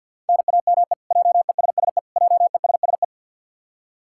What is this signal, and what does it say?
Morse code, heard as a keyed tone, about 51 wpm, 700 Hz. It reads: NAME JESSE JESSE
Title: K1USN SST QSO Practice 50wpm